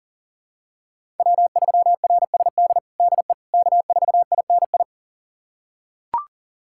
Morse code, heard as a keyed tone, about 40 wpm, 700 Hz. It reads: W3RSD DE K4INI
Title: USA Callsign DE Callsign - 5 Characters - 40wpm